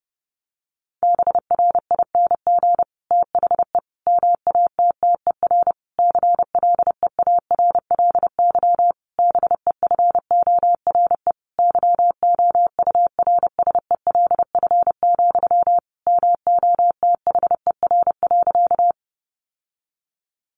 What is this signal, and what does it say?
Morse code, heard as a keyed tone, about 30 wpm, 700 Hz. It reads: BRING THE MATTER CLEARLY BEFORE YOURSELF, MOTHER.
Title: Sentences from Top 1300 Words 30wpm